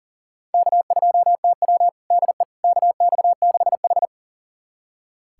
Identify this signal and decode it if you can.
Morse code, heard as a keyed tone, about 40 wpm, 700 Hz. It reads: K2TW DE KX6H
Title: USA Callsign DE Callsign - 4 Characters - 40wpm